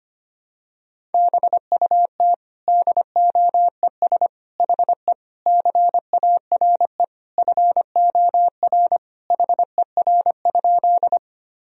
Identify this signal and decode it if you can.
Morse code, heard as a keyed tone, about 25 wpm, 700 Hz. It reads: BUT DOES HE CARE FOR HER?
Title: Sentences from Top 1000 Words - Review Part 1 - 25wpm